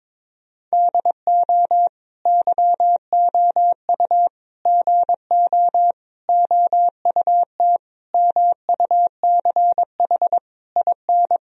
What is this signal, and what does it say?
Morse code, heard as a keyed tone, about 22 wpm, 700 Hz. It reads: DO YOU GO OUT MUCH IN
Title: Sentences from Top 1000 Words - Review Part 4 - 22wpm